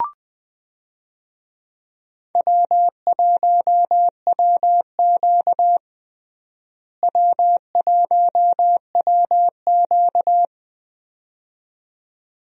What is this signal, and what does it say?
Morse code, heard as a keyed tone, about 20 wpm, 700 Hz. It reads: W1WQ W1WQ
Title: USA Call Signs - 4 Characters - Repeated 2x - 20wpm